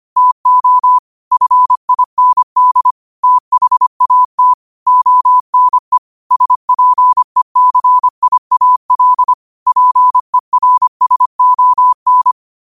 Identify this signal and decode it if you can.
Morse code, heard as a keyed tone, about 25 wpm, 1000 Hz. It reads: TO FIND THAT ONE SPECIAL PERSON
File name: qotdfiles / 25wpm